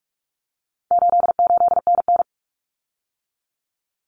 Morse code, heard as a keyed tone, about 44 wpm, 700 Hz. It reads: 88NN
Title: Common Cut Numbers - 4-Digit 45wpm